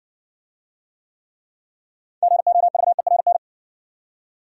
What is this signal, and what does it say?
Morse code, heard as a keyed tone, about 58 wpm, 700 Hz. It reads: GOVERN